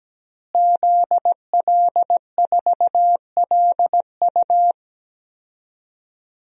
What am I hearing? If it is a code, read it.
Morse code, heard as a keyed tone, about 17 wpm, 700 Hz. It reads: ZL4LU
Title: DXCC-100 Call Signs - 5 Characters - Repeated 3x - 17wpm